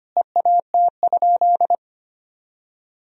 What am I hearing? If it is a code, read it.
Morse code, heard as a keyed tone, about 25 wpm, 700 Hz. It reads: EAT?
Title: Sentences from Top 300 Words and Abbreviations 25wpm